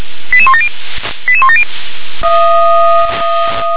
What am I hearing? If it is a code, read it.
Five-tone selective call (ZVEI), heard as quick runs of five tones, tones of about 70 ms. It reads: 88170 88170